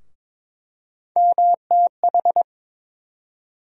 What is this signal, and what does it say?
Morse code, heard as a keyed tone, about 22 wpm, 700 Hz. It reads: MTH
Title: USA Call Sign Postfixes - 3 Letters 22wpm